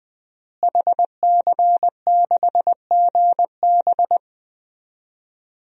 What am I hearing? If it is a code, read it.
Morse code, heard as a keyed tone, about 21 wpm, 700 Hz. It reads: HC6GB
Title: DXCC-100 Call Signs - 5 Characters - Repeated 2x - 20wpm